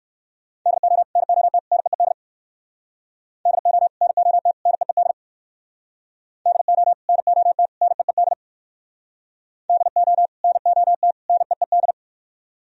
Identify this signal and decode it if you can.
Morse code, heard as a keyed tone, about 64 wpm, 700 Hz. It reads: DO NOT NEED DO NOT NEED DO NOT NEED DO NOT NEED